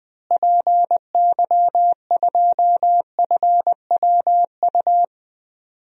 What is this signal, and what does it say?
Morse code, heard as a keyed tone, about 21 wpm, 700 Hz. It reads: PY2FWU
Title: DXCC-100 Call Signs - 6 Characters - Repeated 2x - 20wpm